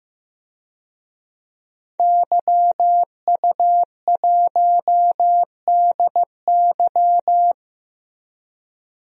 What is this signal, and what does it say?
Morse code, heard as a keyed tone, about 15 wpm, 700 Hz. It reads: YU1DY
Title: DXCC-100 Call Signs - 5 Characters - Repeated 3x - 15wpm